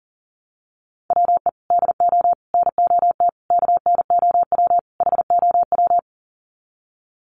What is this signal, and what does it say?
Morse code, heard as a keyed tone, about 40 wpm, 700 Hz. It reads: WE DO NOT KNOW HOW
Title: Sets of 5 Words Encore 40wpm